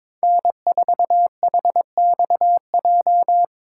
Morse code, heard as a keyed tone, about 22 wpm, 700 Hz. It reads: N4HXJ
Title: USA Call Signs - 5 Characters - 4x Character Spacing - 22wpm